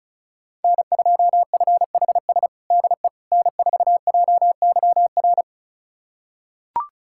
Morse code, heard as a keyed tone, about 36 wpm, 700 Hz. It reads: N2FHS DE N4JYR
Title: USA Callsign DE Callsign - 5 Characters - 35wpm